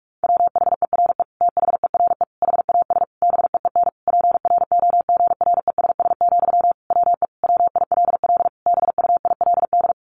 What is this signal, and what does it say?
Morse code, heard as a keyed tone, about 44 wpm, 700 Hz. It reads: WHERE THERE HAS BEEN PROGRESS, WE WILL BUILD
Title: Sentences from Top 1400 Words 45wpm